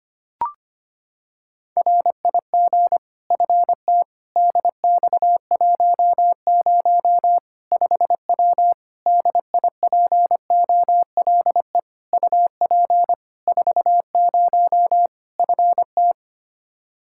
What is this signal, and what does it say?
Morse code, heard as a keyed tone, about 25 wpm, 700 Hz. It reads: RIG FT DX10 5W DIPOLE UP 40 FT
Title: Practice QSO - Exchange 2 - Info - 25wpm